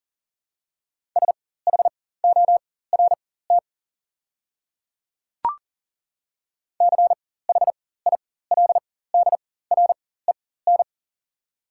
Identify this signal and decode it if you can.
Morse code, heard as a keyed tone, about 39 wpm, spaced as for 18 wpm, 690 Hz. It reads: SHORT CHILDREN
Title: 4x Character Spacing - Top 500 Words - 40wpm